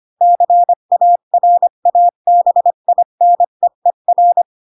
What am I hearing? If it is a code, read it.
Morse code, heard as a keyed tone, about 25 wpm, 700 Hz. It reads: CARABINEER
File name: Pinocchio / Pinocchio ch3 20 25 ik1hge